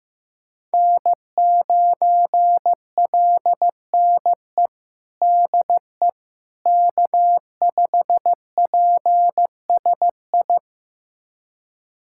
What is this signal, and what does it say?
Morse code, heard as a keyed tone, about 15 wpm, 700 Hz. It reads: N9LNE DE K5PSI
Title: USA Callsign DE Callsign - 5 Characters - 15wpm